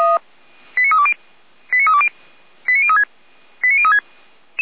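Five-tone selective call (ZVEI), heard as quick runs of five tones, tones of about 70 ms. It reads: 89320 89320 89036 89036
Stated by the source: ZVEI five-tone call received on SDR.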